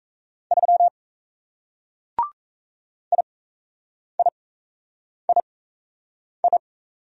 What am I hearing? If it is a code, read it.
Morse code, heard as a keyed tone, about 38 wpm, spaced as for 8 wpm, 700 Hz. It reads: 3 IIII